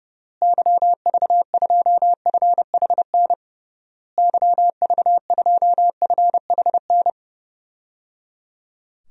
Morse code, heard as a keyed tone, about 30 wpm, 700 Hz. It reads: YV2FHN YV2FHN
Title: DXCC-100 Call Signs - 6 Characters - Repeated 3x - 30wpm